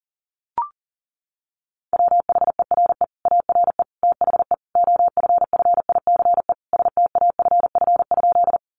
Morse code, heard as a keyed tone, about 40 wpm, 700 Hz. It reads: WHERE ARE THE OFFICE STAFF?
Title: Sentences from Top 1200 Words 40wpm